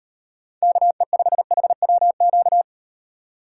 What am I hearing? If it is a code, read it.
Morse code, heard as a keyed tone, about 38 wpm, 700 Hz. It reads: KE5HWQ